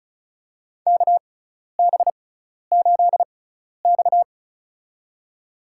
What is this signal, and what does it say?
Morse code, heard as a keyed tone, about 34 wpm, spaced as for 12 wpm, 700 Hz. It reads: KB8X
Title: USA Call Signs - 4 Characters - 6x Character Spacing - 35wpm